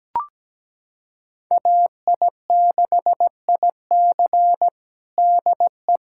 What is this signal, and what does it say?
Morse code, heard as a keyed tone, about 17 wpm, 700 Hz. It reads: AI6IC DE
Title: USA Callsign DE Callsign 17wpm